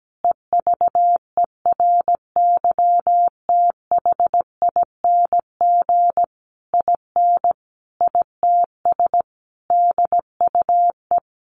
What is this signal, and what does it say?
Morse code, heard as a keyed tone, about 17 wpm, 700 Hz. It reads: EVERYTHING IN ITS DUE
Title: Sentences from Top 900 Words 17wpm